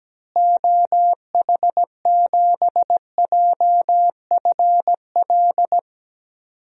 Morse code, heard as a keyed tone, about 17 wpm, 700 Hz. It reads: OH7JFL
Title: DXCC-100 Call Signs - 6 Characters - Repeated 3x - 17wpm